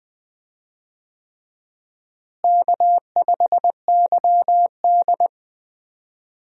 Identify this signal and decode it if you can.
Morse code, heard as a keyed tone, about 21 wpm, 700 Hz. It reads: K5YD
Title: USA Call Signs - 4 Characters - Repeated 3x - 20wpm